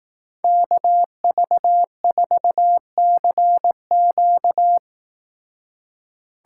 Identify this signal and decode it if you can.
Morse code, heard as a keyed tone, about 18 wpm, 700 Hz. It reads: KV4CQ
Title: Speed Racing - USA Call Signs 15wpm